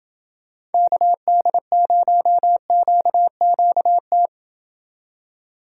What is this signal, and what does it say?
Morse code, heard as a keyed tone, about 29 wpm, 700 Hz. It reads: KD0QQT